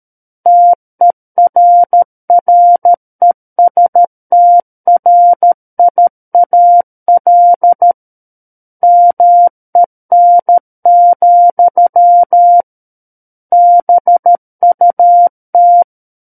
Morse code, heard as a keyed tone, about 13 wpm, 700 Hz. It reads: TERRESTRIAL MEN, BUT
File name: Book One / 04-Chapter 4 - 13 WPM